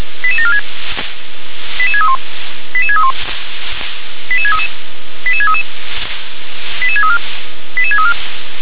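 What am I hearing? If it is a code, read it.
Five-tone selective call (ZVEI), heard as quick runs of five tones, tones of about 70 ms. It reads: 88746 88631 88631 88633 88633 88634 88634
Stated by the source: Five-tone ZVEI selective call 88731 received on an SDR.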